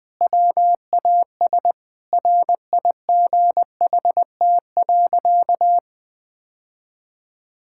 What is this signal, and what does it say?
Morse code, heard as a keyed tone, about 20 wpm, 700 Hz. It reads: WAS RIGHT.
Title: Sentences with 4-Word Phrases - 20wpm